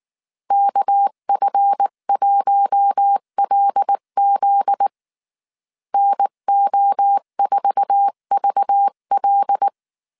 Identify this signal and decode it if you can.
Morse code, heard as a keyed tone, about 19 wpm, 800 Hz. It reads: KF1LZ NO4VL
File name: CW / DET / 1 9 1500C S C 18wpm 2539